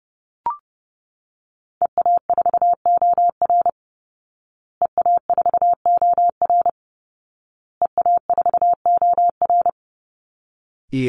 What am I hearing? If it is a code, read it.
Morse code, heard as a keyed tone, about 30 wpm, 700 Hz. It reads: EA4OR EA4OR EA4OR
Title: Call Signs Repeated 2x - 30wpm